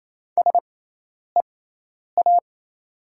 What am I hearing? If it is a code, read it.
Morse code, heard as a keyed tone, about 28 wpm, spaced as for 10 wpm, 690 Hz. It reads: SEA
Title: 6x Character Spacing - Top 500 Words - 28wpm